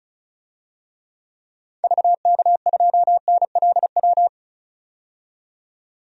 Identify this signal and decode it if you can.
Morse code, heard as a keyed tone, about 34 wpm, 700 Hz. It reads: VK2NLW
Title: DXCC-100 Call Signs - 6 Characters - 35wpm